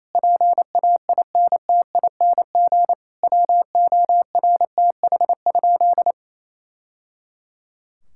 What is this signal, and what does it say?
Morse code, heard as a keyed tone, about 27 wpm, 700 Hz. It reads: PAINTING WORTH?
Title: Sentences from Top 1000 Words - Review Part 6 - 28wpm